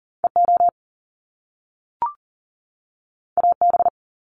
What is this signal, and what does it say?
Morse code, heard as a keyed tone, about 41 wpm, 700 Hz. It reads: EO AB
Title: Top Two Letter Combinations in Words 40wpm